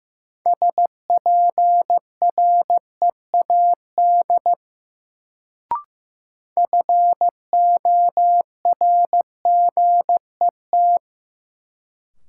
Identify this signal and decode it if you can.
Morse code, heard as a keyed tone, about 15 wpm, 700 Hz. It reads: SPREAD FORGET
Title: Sentences from Top 1200 Words - New Word Drill - 15wpm